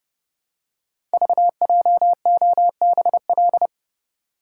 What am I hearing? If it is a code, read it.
Morse code, heard as a keyed tone, about 30 wpm, 700 Hz. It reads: VJOBL
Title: Code Groups - 5 Letters - 30wpm